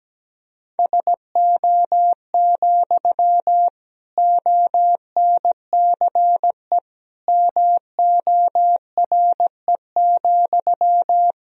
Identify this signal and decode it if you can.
Morse code, heard as a keyed tone, about 17 wpm, 700 Hz. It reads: SO, ONCE MORE,
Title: Sentences from Top 300 Words and Abbreviations 17wpm